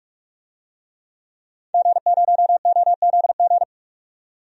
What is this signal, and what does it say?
Morse code, heard as a keyed tone, about 43 wpm, 700 Hz. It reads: G0OZG
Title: DXCC-100 Call Signs - 5 Characters - 45wpm